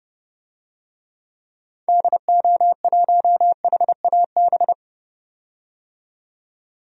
Morse code, heard as a keyed tone, about 30 wpm, 700 Hz. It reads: DO1HAB
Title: DXCC-100 Call Signs - 6 Characters - 30wpm